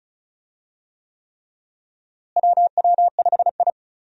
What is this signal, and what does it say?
Morse code, heard as a keyed tone, about 36 wpm, 700 Hz. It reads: WW5I
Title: USA Call Signs - 4 Characters - Repeated 3x - 35wpm